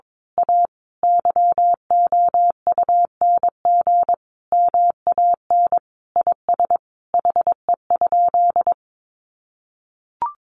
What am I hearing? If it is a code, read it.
Morse code, heard as a keyed tone, about 22 wpm, 700 Hz. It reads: A YOUNG MAN IS HE?
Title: Sentences from Top 1300 Words 22wpm